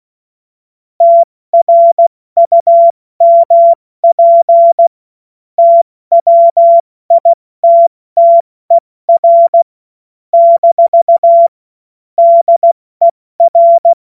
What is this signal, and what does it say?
Morse code, heard as a keyed tone, about 16 wpm, 680 Hz. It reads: TRUMP TWITTER - DER
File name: Morsekurs / Witze / Witz 01. Juni 2021, 70 F80